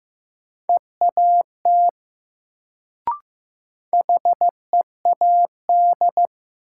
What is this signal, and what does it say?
Morse code, heard as a keyed tone, about 16 wpm, 700 Hz. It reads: EAT HEAD
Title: Sentences from Top 300 Words - New Word Drill - 15wpm